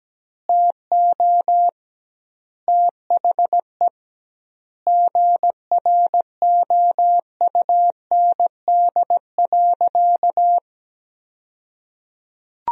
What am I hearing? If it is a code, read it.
Morse code, heard as a keyed tone, about 17 wpm, 700 Hz. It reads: TO THE GROUND.
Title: 2x Word-Spacing - Sentences from Top 500 Words - 17wpm